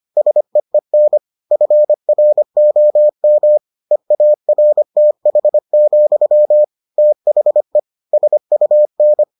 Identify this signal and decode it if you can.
Morse code, heard as a keyed tone, about 25 wpm, 600 Hz. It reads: SEEN FROM EARTH, THE SUN